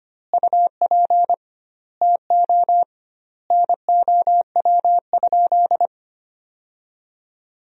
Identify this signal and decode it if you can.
Morse code, heard as a keyed tone, about 25 wpm, 700 Hz. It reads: UP TO NOW?
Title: 2x Word-Spacing - Sentences from Top 100 Words - 25wpm